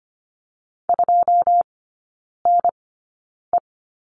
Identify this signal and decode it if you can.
Morse code, heard as a keyed tone, about 25 wpm, spaced as for 9 wpm, 700 Hz. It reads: 2NE
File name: Lesson 14e - Number 2 - Three Characters 25wpm